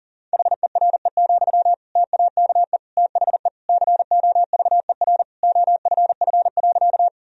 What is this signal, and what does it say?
Morse code, heard as a keyed tone, about 40 wpm, 700 Hz. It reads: HERE, TAKE THE COVER OFF.
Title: Sentences from Top 1000 Words - Review Part 1 - 40wpm